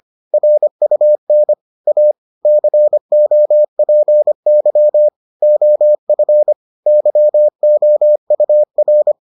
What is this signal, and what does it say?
Morse code, heard as a keyed tone, about 25 wpm, 600 Hz. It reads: RUN A COPY OF YOUR